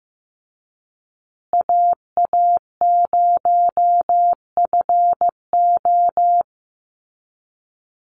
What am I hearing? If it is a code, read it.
Morse code, heard as a keyed tone, about 15 wpm, 700 Hz. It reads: AA0FO
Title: Call Signs Repeated 3x - 15wpm